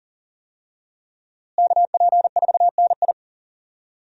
Morse code, heard as a keyed tone, about 40 wpm, 700 Hz. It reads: KP4NI
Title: USA Call Signs - 5 Characters - Repeated 3x - 40wpm